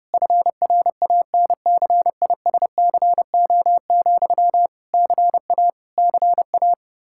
Morse code, heard as a keyed tone, about 30 wpm, 700 Hz. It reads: FRANCISCO, CA CA
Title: K1USN SST QSO Practice 30wpm